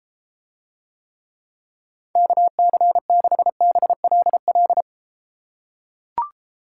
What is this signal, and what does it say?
Morse code, heard as a keyed tone, about 33 wpm, 700 Hz. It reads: KC6BLL